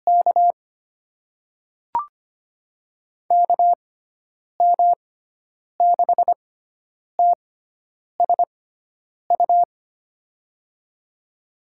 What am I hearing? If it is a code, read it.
Morse code, heard as a keyed tone, about 25 wpm, spaced as for 9 wpm, 700 Hz. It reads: K KM6TSU
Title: USA Call Signs - 6 Characters - 6x Character Spacing - 25wpm